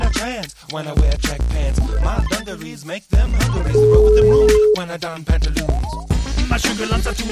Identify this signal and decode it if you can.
Ringback tone (Europe, 425 Hz), with speech.